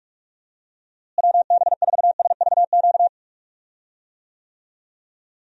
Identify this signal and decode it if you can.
Morse code, heard as a keyed tone, about 46 wpm, 700 Hz. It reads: WB4SVQ
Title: USA Call Signs - 6 Characters - Repeated 3x - 45wpm